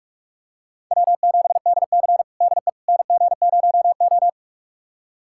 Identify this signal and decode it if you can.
Morse code, heard as a keyed tone, about 46 wpm, 700 Hz. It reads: W7DC DE NG0O
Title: USA Callsign DE Callsign - 4 Characters - 45wpm